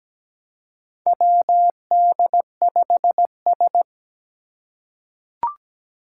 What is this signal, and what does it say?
Morse code, heard as a keyed tone, about 17 wpm, 700 Hz. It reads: WD5S
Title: Four Challenging Letter-Number Combinations 17wpm